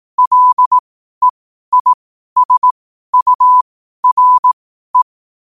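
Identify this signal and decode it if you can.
Morse code, heard as a keyed tone, about 18 wpm, spaced as for 12 wpm, 1000 Hz. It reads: LEISURE
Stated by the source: Morse code at 12 wpm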